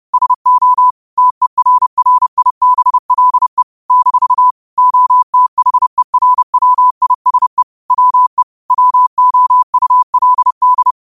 Morse code, heard as a keyed tone, about 30 wpm, 1000 Hz. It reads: SO TERRIBLE - OTHERWISE WE WOULD
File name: qotdfiles / 30wpm